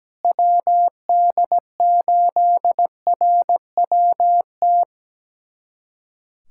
Morse code, heard as a keyed tone, about 17 wpm, 700 Hz. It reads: WD8RWT
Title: USA Callsign DE Callsign 17wpm